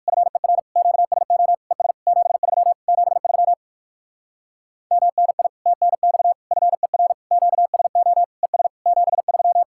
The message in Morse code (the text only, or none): FER QSO ES 73 73 MNI TNX FER QSO ES 73